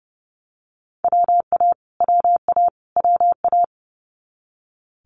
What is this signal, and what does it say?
Morse code, heard as a keyed tone, about 30 wpm, 700 Hz. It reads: WA WA WA
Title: Multi-letter Phonemes Repeated 3x 30wpm